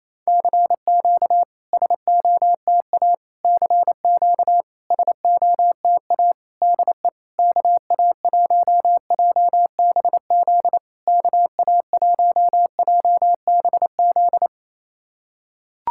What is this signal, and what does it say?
Morse code, heard as a keyed tone, about 28 wpm, 690 Hz. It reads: CQ SOTA CQ SOTA DE KA1JBZ KA1JBZ
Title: SOTA QSO Practice 28wpm